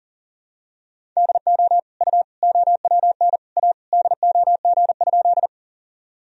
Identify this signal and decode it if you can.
Morse code, heard as a keyed tone, about 40 wpm, 700 Hz. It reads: DO U OWN A DOG?
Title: Sentences from Top 400 Words and Abbreviations 40wpm